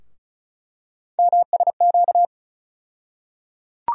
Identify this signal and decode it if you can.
Morse code, heard as a keyed tone, about 36 wpm, 700 Hz. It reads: MSQ
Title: USA Call Sign Postfixes - 3 Letters 35wpm